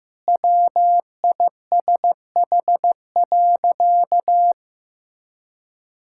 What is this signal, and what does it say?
Morse code, heard as a keyed tone, about 15 wpm, 700 Hz. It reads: WISH.
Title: Sentences from Top 1600 Words 15wpm